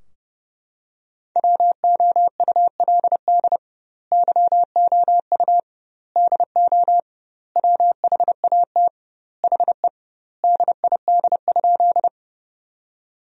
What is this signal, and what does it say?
Morse code, heard as a keyed tone, about 30 wpm, 700 Hz. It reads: WOULD YOU DO WHAT HE DID?
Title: 2x Word-Spacing - Sentences from Top 100 Words - 30wpm